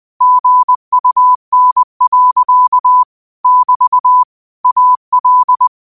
Morse code, heard as a keyed tone, about 20 wpm, 1000 Hz. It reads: GUN. <BT> AL